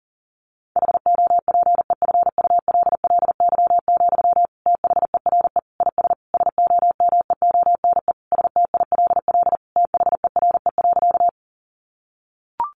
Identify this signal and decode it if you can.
Morse code, heard as a keyed tone, about 40 wpm, 700 Hz. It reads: HOPEFULLY, THERE IS SOMEONE STILL THERE.
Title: Sentences from Top 1500 Words 40wpm